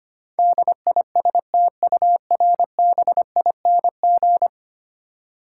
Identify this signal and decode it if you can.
Morse code, heard as a keyed tone, about 25 wpm, 700 Hz. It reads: DISTURBING
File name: 10 Letter Words - First 3 Letters Given - 25wpm